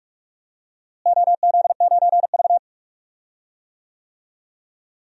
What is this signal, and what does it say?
Morse code, heard as a keyed tone, about 46 wpm, 700 Hz. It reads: OZ9V
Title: DXCC-100 Call Signs - 4 Characters - Repeated 2x - 45wpm